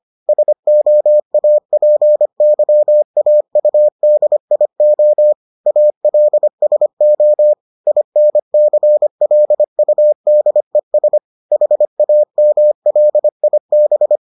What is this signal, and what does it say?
Morse code, heard as a keyed tone, about 25 wpm, 600 Hz. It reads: SOAPYAUDIO ALSO INCLUDES HAMLIB